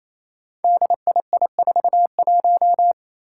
Morse code, heard as a keyed tone, about 28 wpm, 700 Hz. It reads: DII41
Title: Five Challenging Letter-Number Combinations 28wpm